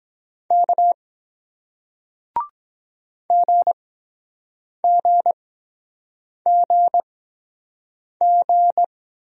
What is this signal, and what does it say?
Morse code, heard as a keyed tone, about 23 wpm, spaced as for 7 wpm, 700 Hz. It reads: K GGGG